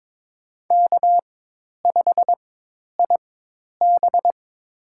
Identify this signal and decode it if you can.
Morse code, heard as a keyed tone, about 22 wpm, spaced as for 10 wpm, 700 Hz. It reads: K5IB
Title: USA Call Signs - 4 Characters - 4x Character Spacing - 22wpm